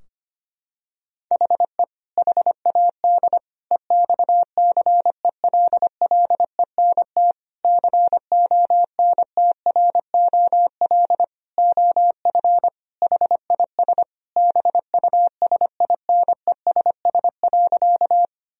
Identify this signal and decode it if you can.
Morse code, heard as a keyed tone, about 25 wpm, 700 Hz. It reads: HE HAD EXCELLENT CONTROL OF HIS BUSINESS.
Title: Sentences from Top 1000 Words - Review Part 3 - 25wpm